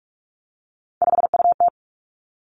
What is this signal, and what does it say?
Morse code, heard as a keyed tone, about 46 wpm, 700 Hz. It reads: 5UT
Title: Three Challenging Letter-Number Combinations 45wpm